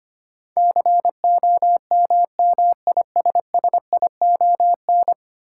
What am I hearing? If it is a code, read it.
Morse code, heard as a keyed tone, about 25 wpm, 700 Hz. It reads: COMMISSION
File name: practice / 25wpm / Ten Letter Words - 25wpm